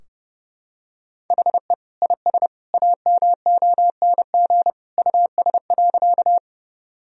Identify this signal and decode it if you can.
Morse code, heard as a keyed tone, about 32 wpm, 700 Hz. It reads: HE IS AMONG US.
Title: Sentences from Top 1000 Words - Review Part 3 - 30wpm